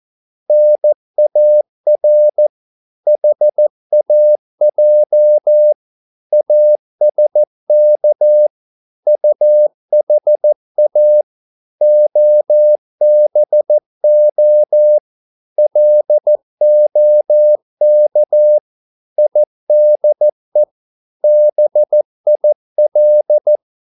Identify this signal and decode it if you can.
Morse code, heard as a keyed tone, about 14 wpm, 600 Hz. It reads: NAR HAJ ASK UHA OBO LOK IDE BIL
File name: Korte ord DK 14wpm